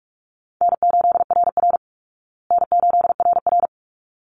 Morse code, heard as a keyed tone, about 44 wpm, 700 Hz. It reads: N8RR N8RR
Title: Easy Call Signs Repeated 2x - 45wpm